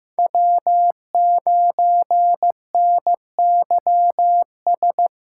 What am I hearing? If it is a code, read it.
Morse code, heard as a keyed tone, about 18 wpm, 700 Hz. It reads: W9NYS
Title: Speed Racing - USA 1x3 Call Signs 15wpm